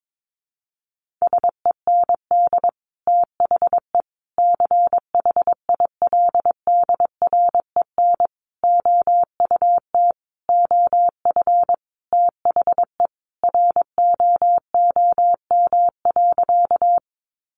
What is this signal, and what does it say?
Morse code, heard as a keyed tone, about 22 wpm, 700 Hz. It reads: SEND THE CHILDREN OUT OF THE ROOM.
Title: Sentences from Top 1500 Words 22wpm